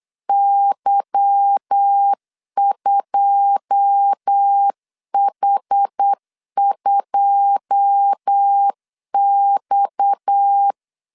Morse code, heard as a keyed tone, about 8 wpm, 800 Hz. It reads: Y2H2X
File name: LEZIONI / 9 2 8wpm 3437